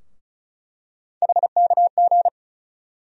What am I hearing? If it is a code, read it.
Morse code, heard as a keyed tone, about 34 wpm, 700 Hz. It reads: HKG
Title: USA Call Sign Postfixes - 3 Letters 35wpm